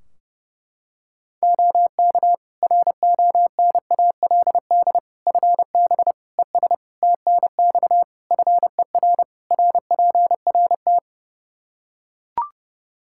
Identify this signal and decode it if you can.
Morse code, heard as a keyed tone, about 30 wpm, 700 Hz. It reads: OK RONALD FB ES TNX FER RPRT
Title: Practice QSO - Exchange 2 - Info - 30wpm